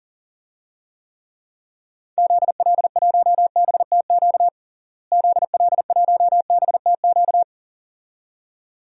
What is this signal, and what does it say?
Morse code, heard as a keyed tone, about 40 wpm, 700 Hz. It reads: ZL1BTQ ZL1BTQ
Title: DXCC-100 Call Signs - 6 Characters - Repeated 2x - 40wpm